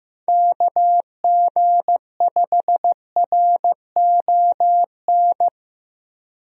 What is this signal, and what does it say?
Morse code, heard as a keyed tone, about 15 wpm, 700 Hz. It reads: KG5RON
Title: USA Call Signs - 6 Characters - 15wpm